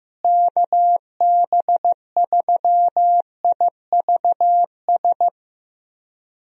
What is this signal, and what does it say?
Morse code, heard as a keyed tone, about 15 wpm, 700 Hz. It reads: KB3IVS
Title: USA Call Signs - 6 Characters - Repeated 2x - 15wpm